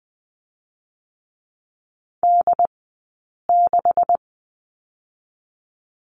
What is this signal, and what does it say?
Morse code, heard as a keyed tone, about 20 wpm, 700 Hz. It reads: D 6
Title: 177 Lesson 29d - Slash - Two Characters 20wpm